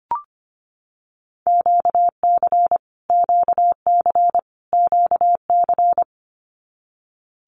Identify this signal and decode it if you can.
Morse code, heard as a keyed tone, about 25 wpm, 700 Hz. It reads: QC QC QC
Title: Canadian Provinces and Territories - Repeated 3x - 25wpm